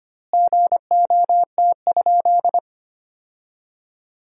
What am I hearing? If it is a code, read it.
Morse code, heard as a keyed tone, about 25 wpm, 700 Hz. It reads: GOT?
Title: 2x Word-Spacing - Sentences from Top 300 Words - 25wpm